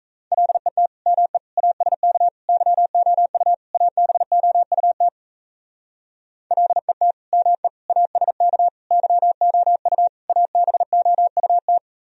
Morse code, heard as a keyed tone, about 35 wpm, 700 Hz. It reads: LET ME ASK YOU ABOUT LET ME ASK YOU ABOUT
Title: Speed-Racing - Sets of 5 Words 28wpm